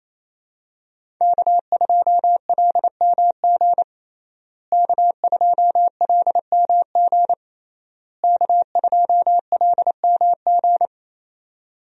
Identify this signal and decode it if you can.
Morse code, heard as a keyed tone, about 27 wpm, 700 Hz. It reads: K2LMG K2LMG K2LMG
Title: USA Call Signs - 5 Characters - Repeated 3x - 28wpm